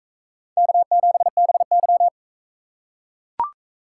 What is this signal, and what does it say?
Morse code, heard as a keyed tone, about 43 wpm, 700 Hz. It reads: K7BY